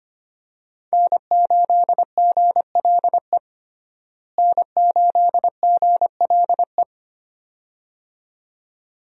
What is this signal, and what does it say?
Morse code, heard as a keyed tone, about 25 wpm, 700 Hz. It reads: N8GLE N8GLE
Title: USA Call Signs - 5 Characters - Repeated 2x - 25wpm